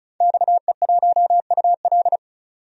Morse code, heard as a keyed tone, about 36 wpm, 700 Hz. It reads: XE1UL
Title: DXCC-100 Call Signs - 5 Characters - 35wpm